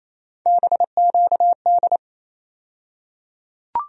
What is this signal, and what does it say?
Morse code, heard as a keyed tone, about 28 wpm, 700 Hz. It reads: BQD
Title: USA Call Sign Postfixes - 3 Letters 28wpm